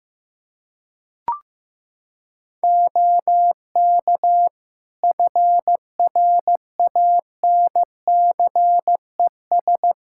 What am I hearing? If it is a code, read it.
Morse code, heard as a keyed tone, about 15 wpm, 700 Hz. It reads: OK FRANCES
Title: Practice QSO - Exchange 3 - More Info - 15wpm